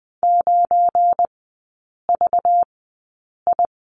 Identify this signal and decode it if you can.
Morse code, heard as a keyed tone, about 20 wpm, spaced as for 8 wpm, 700 Hz. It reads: 9VI
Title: 205 Lesson 33e - Letter j - Three Characters 20wpm